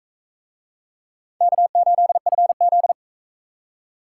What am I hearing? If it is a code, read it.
Morse code, heard as a keyed tone, about 39 wpm, 700 Hz. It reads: K8FZ